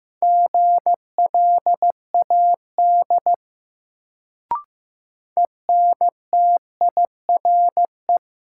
Morse code, heard as a keyed tone, about 15 wpm, 700 Hz. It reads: GLAD ENTIRE
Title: Sentences from Top 800 Words - New Word Drill - 15wpm